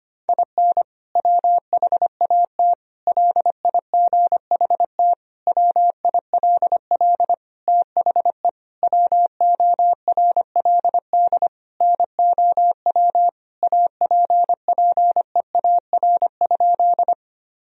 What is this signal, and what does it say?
Morse code, heard as a keyed tone, about 25 wpm, 700 Hz. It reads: IN WHAT LIGHT WILL THE WORLD NOW APPEAR?
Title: Sentences from Top 1000 Words - Review Part 2 - 25wpm